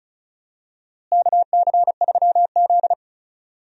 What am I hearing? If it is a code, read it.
Morse code, heard as a keyed tone, about 36 wpm, 700 Hz. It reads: KC3Z
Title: USA Call Signs - 4 Characters - Repeated 3x - 35wpm